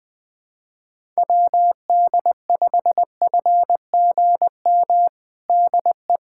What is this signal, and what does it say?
Morse code, heard as a keyed tone, about 20 wpm, 700 Hz. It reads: WD5FGM DE
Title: USA Callsign DE Callsign - 6 Characters - 20wpm